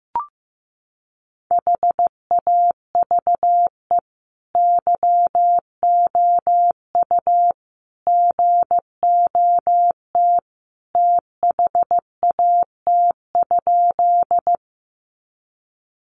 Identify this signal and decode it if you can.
Morse code, heard as a keyed tone, about 15 wpm, 700 Hz. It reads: HAVE YOU GOT THAT?
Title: Sentences from Top 300 Words 15wpm